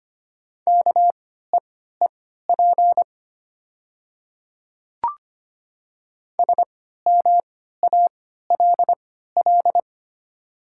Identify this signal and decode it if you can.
Morse code, heard as a keyed tone, about 25 wpm, spaced as for 14 wpm, 700 Hz. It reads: KEEP SMALL
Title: 3x Character Spacing - Top 300 Words - 25wpm